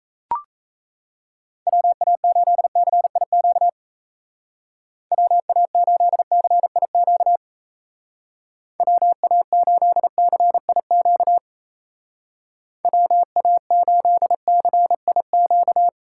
Morse code, heard as a keyed tone, about 34 wpm, 700 Hz. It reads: WA8CIQ WA8CIQ WA8CIQ WA8CIQ
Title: Speed Racing - USA Call Signs 28wpm